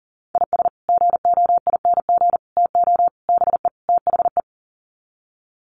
Morse code, heard as a keyed tone, about 40 wpm, 700 Hz. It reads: IS GOING TO BE THE
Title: Sets of 5 Words 40wpm